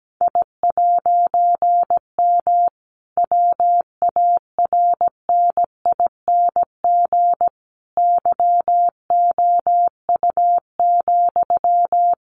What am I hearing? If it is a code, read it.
Morse code, heard as a keyed tone, about 17 wpm, 700 Hz. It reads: I'M WARNING YOU,
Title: A Sound Of Thunder 17wpm